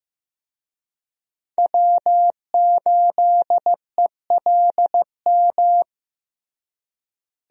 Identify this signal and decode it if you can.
Morse code, heard as a keyed tone, about 15 wpm, 700 Hz. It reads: W8ELM
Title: USA Call Signs - 5 Characters - Repeated 2x - 15wpm